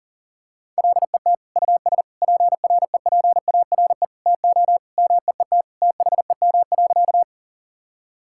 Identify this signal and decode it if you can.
Morse code, heard as a keyed tone, about 37 wpm, 700 Hz. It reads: LET US PREPARE TO MEET THEM.
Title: Sentences from Top 1000 Words - Review Part 6 - 40wpm